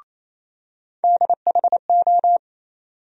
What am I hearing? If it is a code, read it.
Morse code, heard as a keyed tone, about 28 wpm, 700 Hz. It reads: DHO
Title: USA Call Sign Postfixes - 3 Letters 28wpm